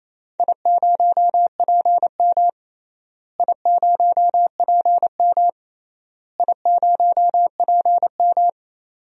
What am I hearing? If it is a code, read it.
Morse code, heard as a keyed tone, about 28 wpm, 700 Hz. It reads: I0PM I0PM I0PM
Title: DXCC-100 Call Signs - 4 Characters - Repeated 3x - 28wpm